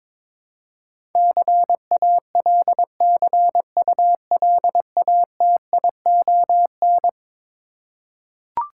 Morse code, heard as a keyed tone, about 22 wpm, 700 Hz. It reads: CALCULATION
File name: practice / 22wpm / Eleven Letter Words - 22wpm